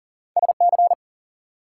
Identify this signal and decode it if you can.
Morse code, heard as a keyed tone, about 54 wpm, 700 Hz. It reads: SC